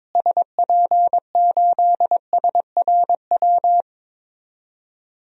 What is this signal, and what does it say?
Morse code, heard as a keyed tone, about 21 wpm, 700 Hz. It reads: SP8SRW
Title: DXCC-100 Call Signs - 6 Characters - Repeated 3x - 22wpm